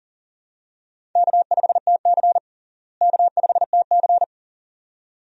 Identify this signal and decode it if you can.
Morse code, heard as a keyed tone, about 40 wpm, 700 Hz. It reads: K5TC K5TC
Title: USA Call Signs - 4 Characters - Repeated 2x - 40wpm